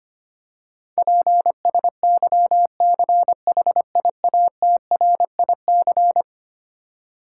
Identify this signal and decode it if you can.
Morse code, heard as a keyed tone, about 25 wpm, 700 Hz. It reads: PSYCHIATRIC
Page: morse-code-ninja-practice-25wpm